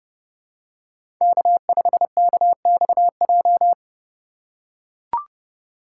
Morse code, heard as a keyed tone, about 31 wpm, 700 Hz. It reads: K5KXJ
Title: K1USN SST QSO Practice - 30wpm